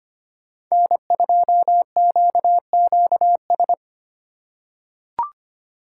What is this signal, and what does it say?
Morse code, heard as a keyed tone, about 25 wpm, 700 Hz. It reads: N2QQS
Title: USA Call Signs - 5 Characters - 6x Character Spacing - 25wpm